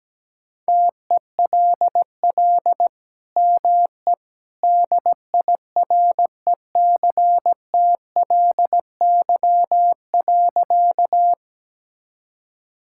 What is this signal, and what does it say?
Morse code, heard as a keyed tone, about 17 wpm, 700 Hz. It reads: TELL ME DIRECTLY.
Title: Sentences from Top 1000 Words - Review Part 5 - 17wpm